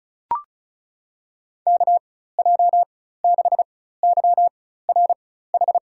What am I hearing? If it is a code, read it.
Morse code, heard as a keyed tone, about 34 wpm, spaced as for 16 wpm, 700 Hz. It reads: KJ6YRH
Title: USA Call Signs - 6 Characters - 4x Character Spacing - 35wpm